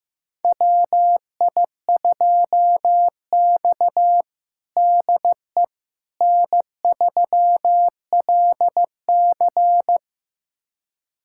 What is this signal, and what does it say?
Morse code, heard as a keyed tone, about 15 wpm, 700 Hz. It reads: WI2X DE N3LC
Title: USA Callsign DE Callsign - 4 Characters - 15wpm